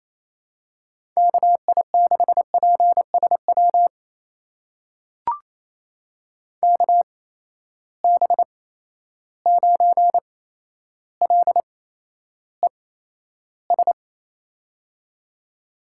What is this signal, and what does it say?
Morse code, heard as a keyed tone, about 28 wpm, spaced as for 12 wpm, 700 Hz. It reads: KI6PSW KB9LES
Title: USA Call Signs - 6 Characters - 8x Character Spacing - 28wpm